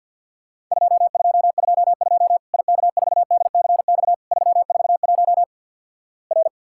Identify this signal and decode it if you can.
Morse code, heard as a keyed tone, about 50 wpm, 700 Hz. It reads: 2222 IK4DCX 341 R
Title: Worked All Europe DX Contest - QTC Exchange - 50wpm